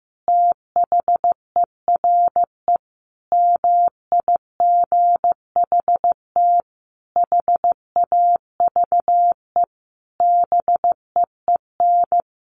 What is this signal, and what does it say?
Morse code, heard as a keyed tone, about 15 wpm, 700 Hz. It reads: THERE MIGHT HAVE BEEN
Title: Sentences from Top 1400 Words 15wpm